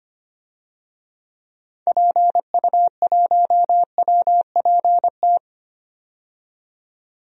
Morse code, heard as a keyed tone, about 25 wpm, 700 Hz. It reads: PU1WPT
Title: DXCC-100 Call Signs - 6 Characters - 25wpm